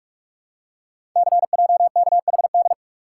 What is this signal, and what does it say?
Morse code, heard as a keyed tone, about 44 wpm, 700 Hz. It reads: CJKHD
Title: Code Groups - 5 Letter:Numbers - 45wpm